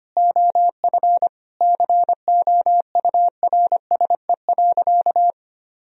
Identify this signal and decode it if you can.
Morse code, heard as a keyed tone, about 25 wpm, 700 Hz. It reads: OF COURSE.
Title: Sentences from Top 1000 Words - Review Part 4 - 25wpm